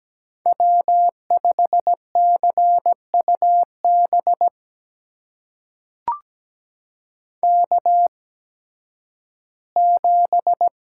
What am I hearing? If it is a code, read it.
Morse code, heard as a keyed tone, about 17 wpm, spaced as for 11 wpm, 700 Hz. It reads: W5CUB K7
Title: USA Call Signs - 5 Characters - 8x Character Spacing - 17wpm